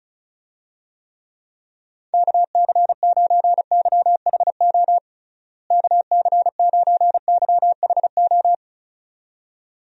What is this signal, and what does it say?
Morse code, heard as a keyed tone, about 34 wpm, 700 Hz. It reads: KC9YHO KC9YHO
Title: USA Call Signs - 6 Characters - Repeated 2x - 35wpm